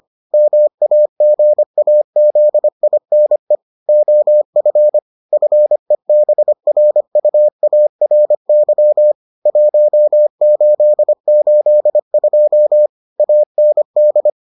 Morse code, heard as a keyed tone, about 25 wpm, 600 Hz. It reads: MAGAZINE OF FEBRUARY 1882 AND